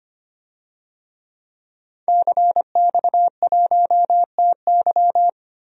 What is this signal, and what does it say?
Morse code, heard as a keyed tone, about 25 wpm, 700 Hz. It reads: CX1TY
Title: DXCC-100 Call Signs - 5 Characters - 25wpm